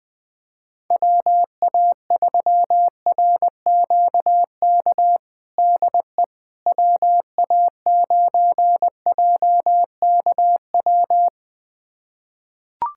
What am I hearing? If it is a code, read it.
Morse code, heard as a keyed tone, about 20 wpm, 700 Hz. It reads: WA3RQK DE WA9JKW
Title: USA Callsign DE Callsign - 6 Characters - 20wpm